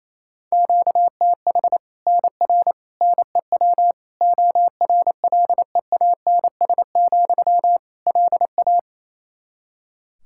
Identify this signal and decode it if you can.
Morse code, heard as a keyed tone, about 28 wpm, 700 Hz. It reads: QTH NR NEW ORLEANS, LA
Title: K1USN SST QSO Practice 28wpm